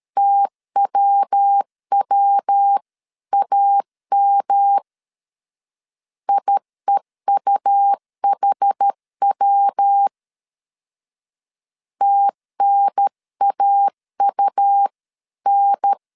Morse code, heard as a keyed tone, about 13 wpm, 800 Hz. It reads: TWWAM IEUHW TNAUN
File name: CW / DET / 1 3 1500C S A 12wpm 2905